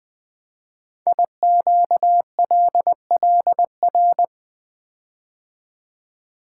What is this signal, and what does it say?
Morse code, heard as a keyed tone, about 20 wpm, 700 Hz. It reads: IQLLR
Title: Code Groups - 5 Letters - 20wpm